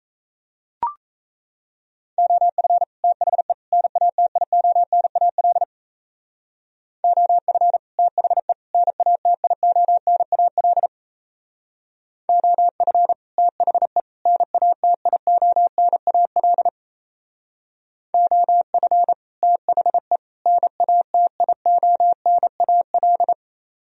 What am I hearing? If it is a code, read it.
Morse code, heard as a keyed tone, about 36 wpm, 700 Hz. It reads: OF THE NATIONAL OF THE NATIONAL OF THE NATIONAL OF THE NATIONAL